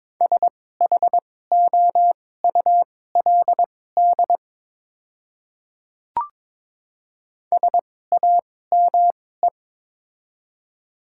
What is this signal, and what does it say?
Morse code, heard as a keyed tone, about 22 wpm, spaced as for 16 wpm, 700 Hz. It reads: SHOULD SAME
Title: 2x Character Spacing - Top 300 Words - 22wpm